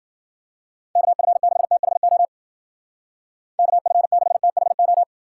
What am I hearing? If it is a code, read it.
Morse code, heard as a keyed tone, about 49 wpm, 700 Hz. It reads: KF6THO KF6THO
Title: Speed Racing - USA Call Signs 40wpm